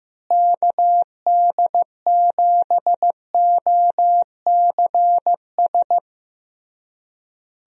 Morse code, heard as a keyed tone, about 15 wpm, 700 Hz. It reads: KD7OCS
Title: USA Call Signs - 6 Characters - Repeated 3x - 15wpm